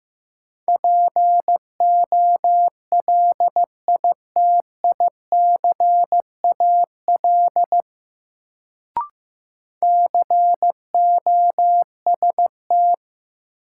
Morse code, heard as a keyed tone, about 15 wpm, 700 Hz. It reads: POLITICAL COST
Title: Sentences from Top 800 Words - New Word Drill - 15wpm